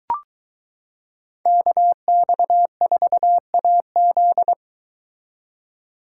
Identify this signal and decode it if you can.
Morse code, heard as a keyed tone, about 22 wpm, 700 Hz. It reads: KX4AZ